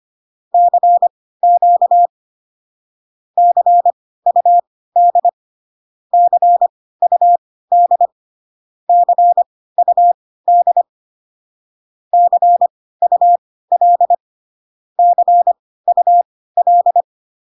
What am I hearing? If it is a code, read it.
Morse code, heard as a keyed tone, about 25 wpm, spaced as for 16 wpm, 700 Hz. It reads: CQ CUD CUD CUD CUL CUL